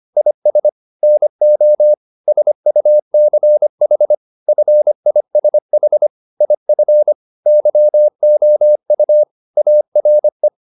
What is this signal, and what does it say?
Morse code, heard as a keyed tone, about 25 wpm, 600 Hz. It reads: IS NO SUCH FISH IF YOU ARE
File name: the old man and the sea part 1 25wpm